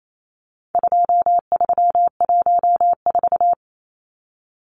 Morse code, heard as a keyed tone, about 28 wpm, 700 Hz. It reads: 2314
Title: Numbers - 4-Digit 28wpm